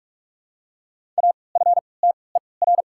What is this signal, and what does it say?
Morse code, heard as a keyed tone, about 48 wpm, spaced as for 26 wpm, 690 Hz. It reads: AFTER
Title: 3x Character Spacing - Top 300 Words - 45wpm